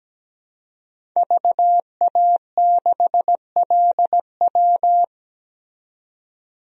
Morse code, heard as a keyed tone, about 17 wpm, 700 Hz. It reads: VA6LW
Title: DXCC-100 Call Signs - 5 Characters - 17wpm